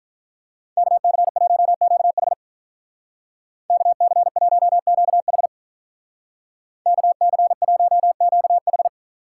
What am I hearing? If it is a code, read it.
Morse code, heard as a keyed tone, about 45 wpm, 700 Hz. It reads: KC1QH KC1QH KC1QH